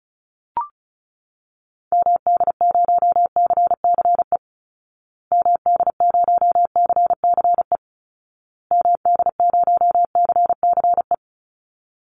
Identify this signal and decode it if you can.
Morse code, heard as a keyed tone, about 34 wpm, 700 Hz. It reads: MD0CCE MD0CCE MD0CCE
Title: Call Signs Repeated 2x - 35wpm